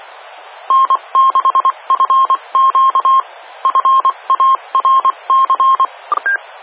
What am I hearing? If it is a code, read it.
Morse code, heard as a keyed tone, about 24 wpm, 1060 Hz. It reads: N6FQ FARC
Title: Fallbrook Amateur Radio Club (FARC) Repeater N6FQ